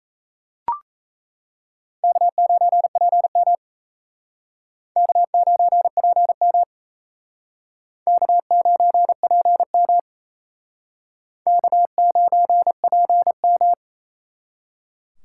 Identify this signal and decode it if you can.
Morse code, heard as a keyed tone, about 35 wpm, 700 Hz. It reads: K9PM K9PM K9PM K9PM
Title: Speed Racing - USA 1x2 Call Signs 28wpm